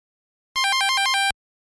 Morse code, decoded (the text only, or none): none